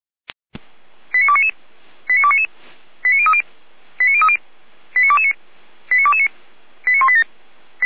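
Five-tone selective call (ZVEI), heard as quick runs of five tones, tones of about 70 ms. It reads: 89299 89299 89030 89030 89229 89229 89187
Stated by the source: ZVEI five-tone selective call received on SDR, from the Feuerwehr Orlamünde fire brigade audio archive.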